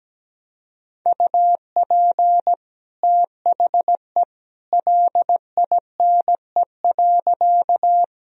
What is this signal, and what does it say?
Morse code, heard as a keyed tone, about 17 wpm, 700 Hz. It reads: UP THE LINE.
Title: Sentences from Top 1000 Words - Review Part 1 - 17wpm